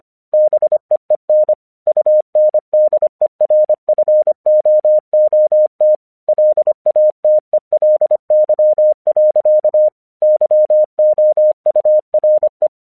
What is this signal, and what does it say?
Morse code, heard as a keyed tone, about 26 wpm, 610 Hz. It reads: BEEN UNDERFOOT LATELY. YOURE
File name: 1 English 25wpm / AQC21